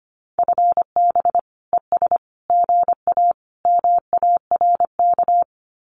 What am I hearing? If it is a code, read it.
Morse code, heard as a keyed tone, about 25 wpm, 700 Hz. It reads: FB ES GA MARK
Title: Common QSO Phrases Encore 25wpm